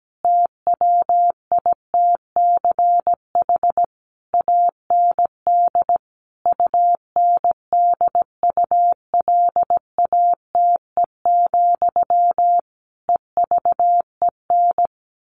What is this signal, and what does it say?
Morse code, heard as a keyed tone, about 17 wpm, 700 Hz. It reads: TWITCH AND UNDULATE, EVEN
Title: A Sound Of Thunder 17wpm